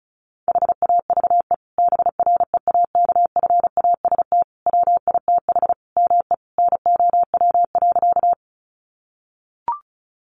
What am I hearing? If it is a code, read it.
Morse code, heard as a keyed tone, about 34 wpm, 700 Hz. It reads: HAVE BREAKFAST WITH ME NOW.
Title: Sentences from Top 1500 Words 35wpm